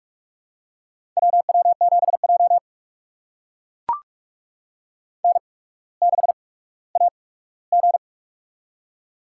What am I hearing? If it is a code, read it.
Morse code, heard as a keyed tone, about 44 wpm, spaced as for 19 wpm, 700 Hz. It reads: WW7J N6AG
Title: USA Call Signs - 4 Characters - 8x Character Spacing - 45wpm